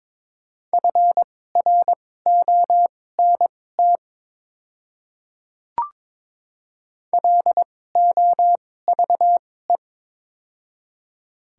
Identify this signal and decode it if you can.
Morse code, heard as a keyed tone, about 22 wpm, spaced as for 16 wpm, 700 Hz. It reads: FRONT LOVE
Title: 2x Character Spacing - Top 500 Words - 22wpm